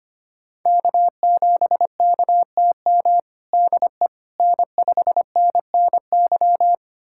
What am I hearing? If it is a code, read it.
Morse code, heard as a keyed tone, about 25 wpm, 700 Hz. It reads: K7KTM DE N5NNY
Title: USA Callsign DE Callsign 25wpm